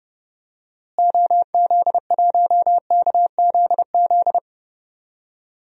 Morse code, heard as a keyed tone, about 31 wpm, 700 Hz. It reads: OZ1KZZ
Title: DXCC-100 Call Signs - 6 Characters - Repeated 2x - 30wpm